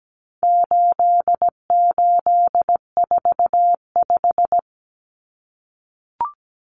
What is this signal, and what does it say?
Morse code, heard as a keyed tone, about 17 wpm, 700 Hz. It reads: 8845
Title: Numbers - 4-Digit 17wpm